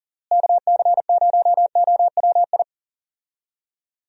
Morse code, heard as a keyed tone, about 40 wpm, 700 Hz. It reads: KC0OWI
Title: USA Call Signs - 6 Characters - Repeated 3x - 40wpm